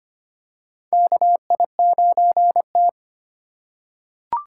Morse code, heard as a keyed tone, about 25 wpm, 700 Hz. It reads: KI9T
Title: USA Call Signs - 4 Characters - 3x Character Spacing - 25wpm